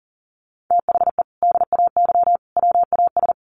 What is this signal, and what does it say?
Morse code, heard as a keyed tone, about 40 wpm, 700 Hz. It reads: THE DAY WAS
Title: Sentences from Top 1200 Words 40wpm